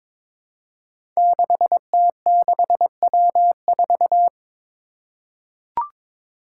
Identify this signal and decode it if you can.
Morse code, heard as a keyed tone, about 22 wpm, 700 Hz. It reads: 6T6W4
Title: Five Challenging Letter-Number Combinations 22wpm